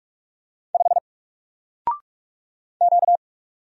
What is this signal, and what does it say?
Morse code, heard as a keyed tone, about 46 wpm, 700 Hz. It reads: 5 Q
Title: Single Letter-Number 45wpm